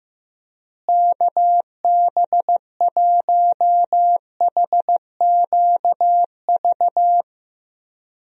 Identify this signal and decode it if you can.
Morse code, heard as a keyed tone, about 15 wpm, 700 Hz. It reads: KB1HQV
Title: USA Call Signs - 6 Characters - Repeated 2x - 15wpm